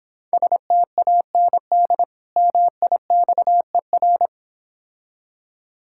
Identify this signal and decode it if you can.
Morse code, heard as a keyed tone, about 26 wpm, 700 Hz. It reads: STAND MIXER